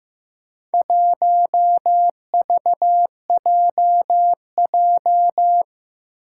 Morse code, heard as a keyed tone, about 15 wpm, 700 Hz. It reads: 1VJJ
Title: Four Challenging Letter-Number Combinations 15wpm